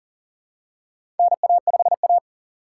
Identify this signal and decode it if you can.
Morse code, heard as a keyed tone, about 40 wpm, 700 Hz. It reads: NA5A
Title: USA Call Signs - 4x Character Spacing - 40wpm